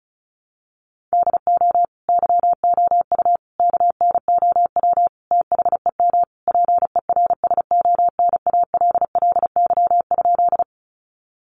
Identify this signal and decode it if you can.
Morse code, heard as a keyed tone, about 34 wpm, 700 Hz. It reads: DO YOU KNOW THEM PERSONALLY?
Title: Sentences from Top 1400 Words 35wpm